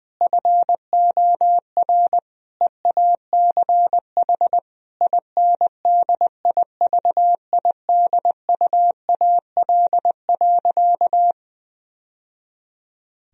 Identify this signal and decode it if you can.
Morse code, heard as a keyed tone, about 20 wpm, 700 Hz. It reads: FOR EACH INDIVIDUAL.
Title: Sentences from Top 1000 Words - Review Part 4 - 20wpm